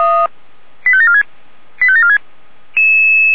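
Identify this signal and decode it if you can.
Five-tone selective call (ZVEI), heard as quick runs of five tones, tones of about 70 ms. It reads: 85746 85746
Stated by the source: ZVEI five-tone selective call received with an SDR.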